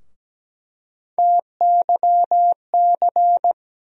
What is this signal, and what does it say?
Morse code, heard as a keyed tone, about 17 wpm, 700 Hz. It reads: TYC
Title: USA Call Sign Postfixes - 3 Letters 17wpm